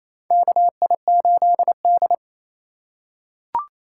Morse code, heard as a keyed tone, about 27 wpm, 700 Hz. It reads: KI8D
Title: USA Call Signs - 4 Characters - 4x Character Spacing - 28wpm